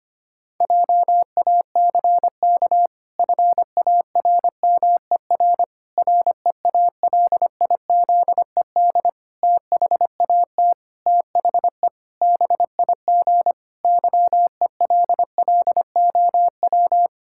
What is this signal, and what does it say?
Morse code, heard as a keyed tone, about 25 wpm, 700 Hz. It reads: JACK FARMER REALIZED THAT THE BIG YELLOW
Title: Pangram Challenge v2 - 25wpm